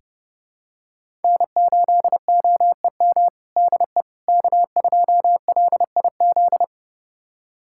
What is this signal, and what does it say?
Morse code, heard as a keyed tone, about 30 wpm, 700 Hz. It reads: N8OEM DE K2LIZ
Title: USA Callsign DE Callsign 30wpm